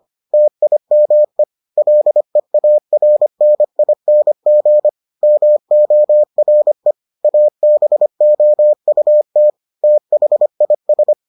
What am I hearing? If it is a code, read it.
Morse code, heard as a keyed tone, about 25 wpm, 600 Hz. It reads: TIME LEARNING MORE ABOUT THIS